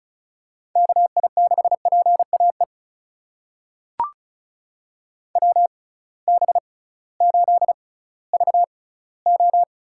Morse code, heard as a keyed tone, about 36 wpm, spaced as for 19 wpm, 700 Hz. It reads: KI6PAE WB8VO
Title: USA Call Signs - 6 Characters - 6x Character Spacing - 35wpm